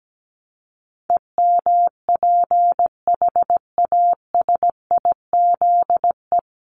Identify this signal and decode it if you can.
Morse code, heard as a keyed tone, about 17 wpm, 700 Hz. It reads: EMPHASIZE
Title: Top 1251-1500 Words 17wpm